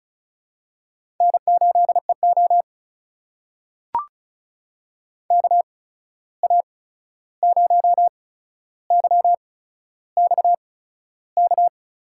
Morse code, heard as a keyed tone, about 36 wpm, spaced as for 13 wpm, 700 Hz. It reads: N8EO KA0YXK
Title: USA Call Signs - 8x Character Spacing - 35wpm